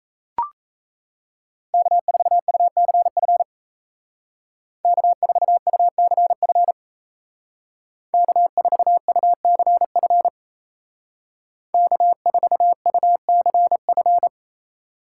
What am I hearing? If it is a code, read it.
Morse code, heard as a keyed tone, about 36 wpm, 700 Hz. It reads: K4UCF K4UCF K4UCF K4UCF